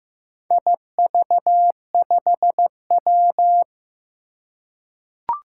Morse code, heard as a keyed tone, about 15 wpm, 700 Hz. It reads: IV5W
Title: Four Challenging Letter-Number Combinations 15wpm